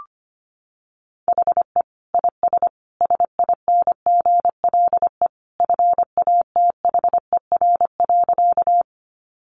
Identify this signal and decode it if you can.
Morse code, heard as a keyed tone, about 25 wpm, 700 Hz. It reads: HE IS SINGLE FATHER.
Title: Sentences from Top 700 Words 25wpm